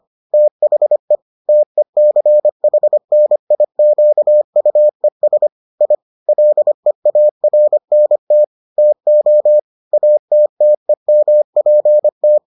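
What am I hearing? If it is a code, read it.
Morse code, heard as a keyed tone, about 25 wpm, 600 Hz. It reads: THE TECHNIQUES I LEARNT TO ATTEMPT
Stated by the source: Morse edition of the Foundations of Amateur Radio episodes.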